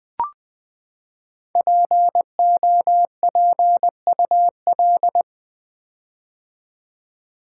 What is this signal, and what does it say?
Morse code, heard as a keyed tone, about 20 wpm, 700 Hz. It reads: POPUL
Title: Build A Word - 5 Letters At A Time - 10-Letter Words - 20wpm